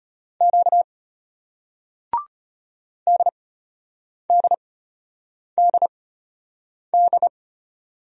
Morse code, heard as a keyed tone, about 34 wpm, spaced as for 8 wpm, 700 Hz. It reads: Q DDDD